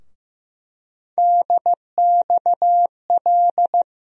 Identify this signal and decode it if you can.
Morse code, heard as a keyed tone, about 15 wpm, 700 Hz. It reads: DXL
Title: USA Call Sign Postfixes - 3 Letters 15wpm